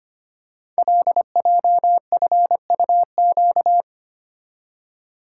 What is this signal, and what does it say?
Morse code, heard as a keyed tone, about 25 wpm, 700 Hz. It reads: LJFUQ
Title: Code Groups - 5 Letters - 25wpm